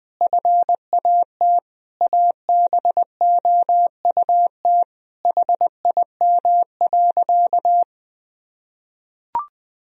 Morse code, heard as a keyed tone, about 20 wpm, 700 Hz. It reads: FAT ABOUT HIM.
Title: Sentences from Top 1000 Words - Review Part 4 - 20wpm